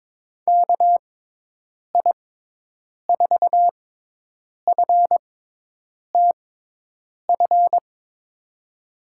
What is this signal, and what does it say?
Morse code, heard as a keyed tone, about 22 wpm, spaced as for 8 wpm, 700 Hz. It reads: KI4FTF
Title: USA Call Signs - 6 Characters - 6x Character Spacing - 22wpm